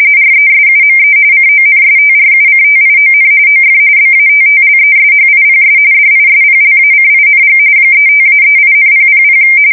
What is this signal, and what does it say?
RTTY, about 45.45 baud. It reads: RESUME CONSTRUCTION OF BATTLESHIPS DISPLACING 35-38,000
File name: fsk170 / battleshipnj